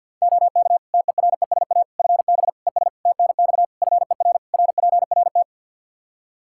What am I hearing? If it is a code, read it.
Morse code, heard as a keyed tone, about 52 wpm, 700 Hz. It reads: OK TERESA FB ES TNX FER RPRT
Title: Practice QSO - Exchange 2 - Info - 50wpm